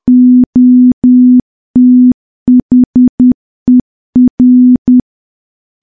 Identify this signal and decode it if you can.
Morse code, heard as a keyed tone, about 10 wpm, 250 Hz. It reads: OTHER
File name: morse / morse-training-10WPM